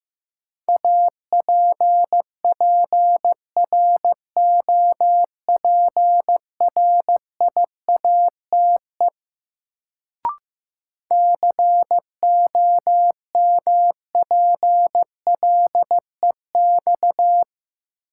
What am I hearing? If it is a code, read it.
Morse code, heard as a keyed tone, about 15 wpm, 700 Hz. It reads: APPROPRIATE COMPLEX
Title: Sentences from Top 900 Words - New Word Drill - 15wpm